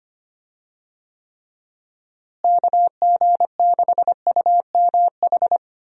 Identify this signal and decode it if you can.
Morse code, heard as a keyed tone, about 25 wpm, 700 Hz. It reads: KG6UMH
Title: USA Call Signs - 6 Characters - Repeated 2x - 25wpm